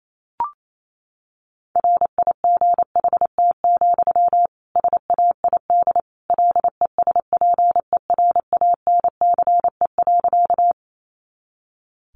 Morse code, heard as a keyed tone, about 28 wpm, 700 Hz. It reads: RIGHT, SAID LESPERANCE.
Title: A Sound Of Thunder 28wpm